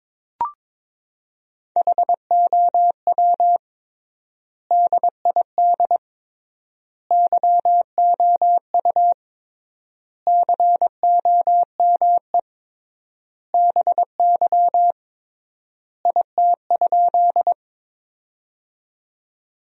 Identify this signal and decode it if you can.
Morse code, heard as a keyed tone, about 22 wpm, 700 Hz. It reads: HOW DID YOU COME BY IT?
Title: 3x Word-Spacing - Sentences from Top 100 Words - 22wpm